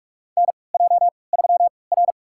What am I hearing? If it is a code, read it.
Morse code, heard as a keyed tone, about 46 wpm, spaced as for 25 wpm, 700 Hz. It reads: NJ3R
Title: USA Call Signs - 3x Character Spacing - 45wpm